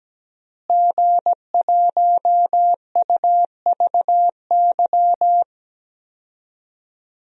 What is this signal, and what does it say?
Morse code, heard as a keyed tone, about 17 wpm, 700 Hz. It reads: G1UVY
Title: DXCC-100 Call Signs - 5 Characters - Repeated 2x - 17wpm